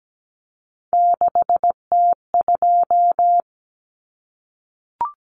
Morse code, heard as a keyed tone, about 17 wpm, 700 Hz. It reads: 6T2
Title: Common Cut Numbers - 3-Digit 17wpm